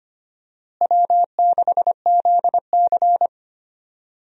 Morse code, heard as a keyed tone, about 25 wpm, 700 Hz. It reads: W6ZC
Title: USA Call Signs - 4 Characters - Repeated 2x - 25wpm